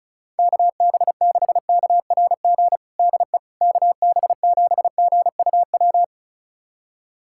Morse code, heard as a keyed tone, about 34 wpm, 690 Hz. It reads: KB6KRG DE KB7GUW
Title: USA Callsign DE Callsign - 6 Characters - 35wpm